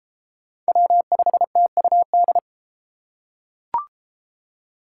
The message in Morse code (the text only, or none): W5TUD